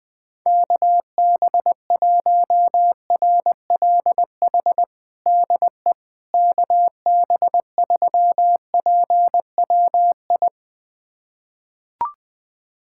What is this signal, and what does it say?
Morse code, heard as a keyed tone, about 20 wpm, 700 Hz. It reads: KB1RLH DE KB3PWI
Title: USA Callsign DE Callsign 20wpm